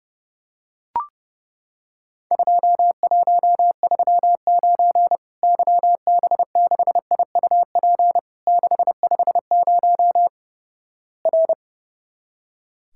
Morse code, heard as a keyed tone, about 30 wpm, 700 Hz. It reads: 2139 YB6IUP 650 R
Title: Worked All Europe DX Contest - QTC Exchange - 30wpm